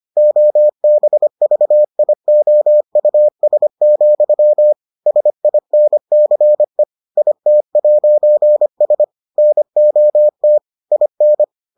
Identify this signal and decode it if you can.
Morse code, heard as a keyed tone, about 25 wpm, 600 Hz. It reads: OBVIOUS, SINCE IT'S NOT IN